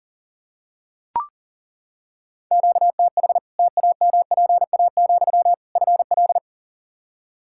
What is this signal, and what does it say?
Morse code, heard as a keyed tone, about 40 wpm, 700 Hz. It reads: QTH TAMPA, FL
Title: Practice QSO - Exchange 1 - Beginning - 40wpm